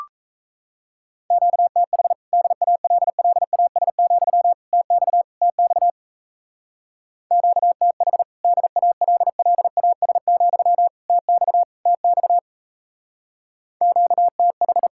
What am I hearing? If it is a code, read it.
Morse code, heard as a keyed tone, about 34 wpm, 700 Hz. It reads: QTH DALLAS, TX TX QTH DALLAS, TX TX QTH
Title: Speed-Racing Common QSO Phrases Encore - 28wpm